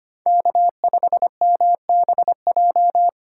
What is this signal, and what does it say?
Morse code, heard as a keyed tone, about 25 wpm, 700 Hz. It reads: K5MBJ
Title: USA Call Signs - 5 Characters - 4x Character Spacing - 25wpm